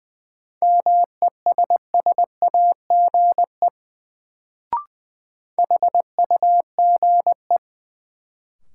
Morse code, heard as a keyed tone, about 20 wpm, 700 Hz. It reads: MESSAGE HUGE
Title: Sentences from Top 1100 Words - New Word Drill Alt - 20wpm